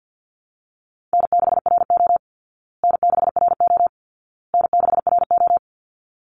Morse code, heard as a keyed tone, about 48 wpm, 700 Hz. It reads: N6RO N6RO N6RO
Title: Easy Call Signs Repeated 3x - 50wpm